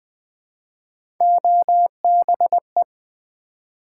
Morse code, heard as a keyed tone, about 20 wpm, 700 Hz. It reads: OBE
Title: Build A Word - 1 Letter At A Time - 4-Letter Words - 20wpm